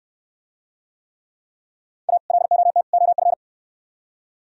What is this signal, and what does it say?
Morse code, heard as a keyed tone, about 66 wpm, 700 Hz. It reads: A LOT OF